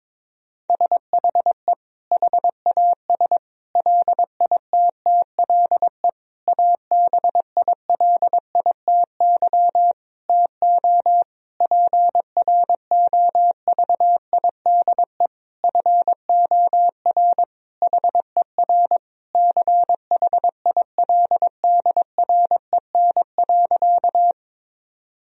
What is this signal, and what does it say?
Morse code, heard as a keyed tone, about 22 wpm, 700 Hz. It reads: SHE HAS LITTLE ABILITY TO PROVIDE FOR HER CHILDREN.
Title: Sentences from Top 1000 Words - Review Part 3 - 22wpm